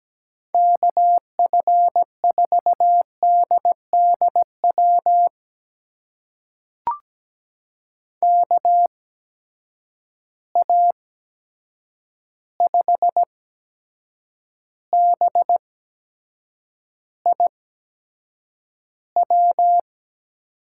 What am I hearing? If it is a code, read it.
Morse code, heard as a keyed tone, about 17 wpm, spaced as for 7 wpm, 700 Hz. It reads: KF4DDW KA5BIW
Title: USA Call Signs - 6 Characters - 8x Character Spacing - 17wpm